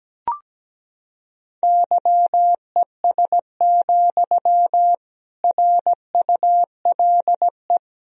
Morse code, heard as a keyed tone, about 17 wpm, 700 Hz. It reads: YES, RULE
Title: Sentences from Top 1000 Words - Review Part 2 - 17wpm